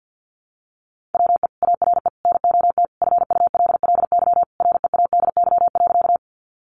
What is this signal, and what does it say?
Morse code, heard as a keyed tone, about 50 wpm, 700 Hz. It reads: WE ARE NOT FULLY READY.
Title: Sentences from Top 1300 Words 50wpm